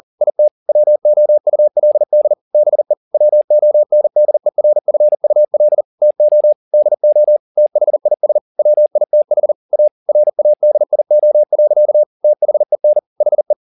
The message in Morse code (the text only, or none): IT WOULD BE WONDERFUL TO DO THIS WITH A RADIO. THEN HE